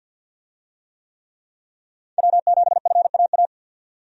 Morse code, heard as a keyed tone, about 52 wpm, 700 Hz. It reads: W7FAA
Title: USA Call Signs - 5 Characters - 50wpm